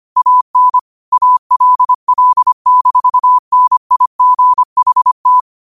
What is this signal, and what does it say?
Morse code, heard as a keyed tone, about 25 wpm, 1000 Hz. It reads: AN ALL-NIGHT